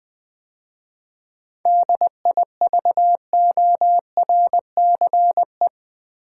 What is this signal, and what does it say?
Morse code, heard as a keyed tone, about 20 wpm, 700 Hz. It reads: DIVORCE
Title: 7 Letter Words - First 3 Letters Given - 20wpm